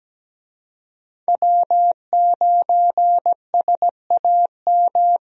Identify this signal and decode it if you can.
Morse code, heard as a keyed tone, about 17 wpm, 700 Hz. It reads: W9SAM
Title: USA Call Signs - 5 Characters - 17wpm